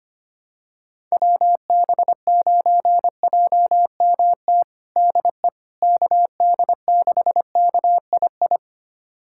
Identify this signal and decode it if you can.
Morse code, heard as a keyed tone, about 25 wpm, 700 Hz. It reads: WB9JMT DE KD6KII
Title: USA Callsign DE Callsign - 6 Characters - 25wpm